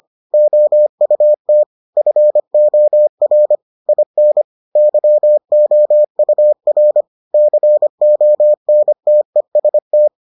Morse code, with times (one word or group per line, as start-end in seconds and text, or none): 0.3-1.6 OUT
2.0-3.5 FOR
3.9-4.4 IN
4.8-7.0 YOUR
7.3-10.1 CONTEST